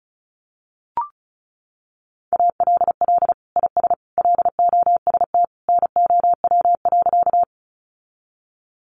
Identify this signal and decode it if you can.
Morse code, heard as a keyed tone, about 34 wpm, 700 Hz. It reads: ALL IS LOST NOW.
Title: Sentences from Top 1400 Words 35wpm